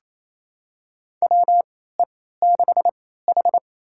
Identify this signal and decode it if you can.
Morse code, heard as a keyed tone, about 28 wpm, spaced as for 16 wpm, 700 Hz. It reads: WE6H
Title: USA Call Signs - 4 Characters - 3x Character Spacing - 28wpm